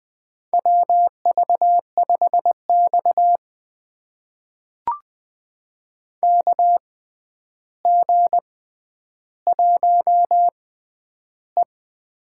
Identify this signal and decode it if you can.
Morse code, heard as a keyed tone, about 20 wpm, spaced as for 10 wpm, 700 Hz. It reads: WV5X KG1E
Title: USA Call Signs - 4 Characters - 6x Character Spacing - 20wpm